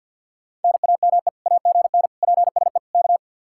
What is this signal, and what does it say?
Morse code, heard as a keyed tone, about 65 wpm, spaced as for 57 wpm, 700 Hz. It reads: NAME AGN PSE K
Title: Speed-Racing Common QSO Phrases - 50wpm